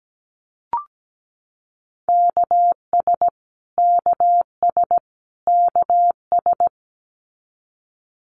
Morse code, heard as a keyed tone, about 17 wpm, 700 Hz. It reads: KS KS KS
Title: US States Repeated 3x 17wpm